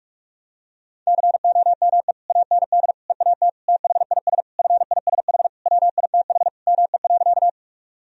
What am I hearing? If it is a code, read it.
Morse code, heard as a keyed tone, about 46 wpm, 700 Hz. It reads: COME AND EAT THIS FISH WITH ME.
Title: Sentences from Top 1000 Words - Review Part 1 - 45wpm